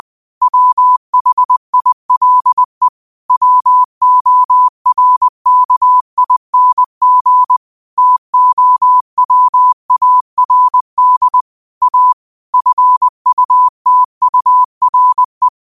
Morse code, heard as a keyed tone, about 20 wpm, 1000 Hz. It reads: WHILE WORKING TOWARD A FUTURE
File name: 20wpm